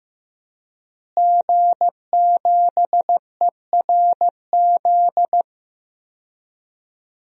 Morse code, heard as a keyed tone, about 15 wpm, 700 Hz. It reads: G7ERZ
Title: DXCC-100 Call Signs - 5 Characters - Repeated 3x - 15wpm